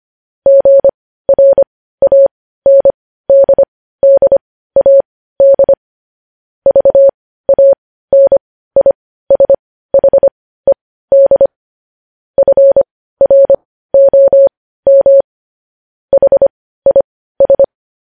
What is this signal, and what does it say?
Morse code, heard as a keyed tone, about 25 wpm, spaced as for 15 wpm, 550 Hz. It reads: GRANDDAD VANISHED FROM HIS
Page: News Headlines in Morse Code at 15 WPM